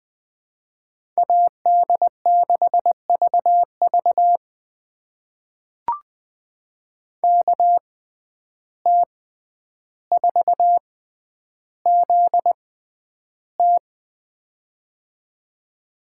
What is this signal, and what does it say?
Morse code, heard as a keyed tone, about 20 wpm, spaced as for 10 wpm, 700 Hz. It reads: AD6VV KT4ZT
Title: USA Call Signs - 5 Characters - 6x Character Spacing - 20wpm